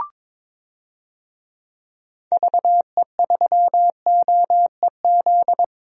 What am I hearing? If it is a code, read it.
Morse code, heard as a keyed tone, about 22 wpm, 700 Hz. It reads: VE3OEZ
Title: DXCC-100 Call Signs - 6 Characters - 22wpm